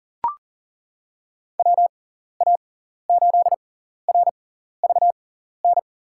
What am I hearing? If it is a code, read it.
Morse code, heard as a keyed tone, about 40 wpm, spaced as for 14 wpm, 700 Hz. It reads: WA8RVN
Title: USA Call Signs - 6 Characters - 6x Character Spacing - 40wpm